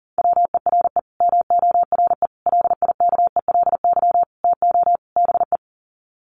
Morse code, heard as a keyed tone, about 40 wpm, 700 Hz. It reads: WERE MORE LIKELY TO BE
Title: Sets of 5 Words Encore2 40wpm